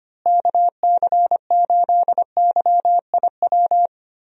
Morse code, heard as a keyed tone, about 25 wpm, 700 Hz. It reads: KC8YIW
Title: USA Call Signs - 6 Characters - Repeated 3x - 25wpm